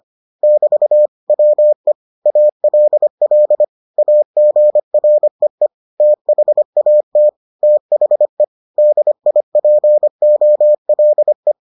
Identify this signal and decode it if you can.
Morse code, heard as a keyed tone, about 25 wpm, 600 Hz. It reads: <BT> WE ALL AGREE THAT THE DIPOLE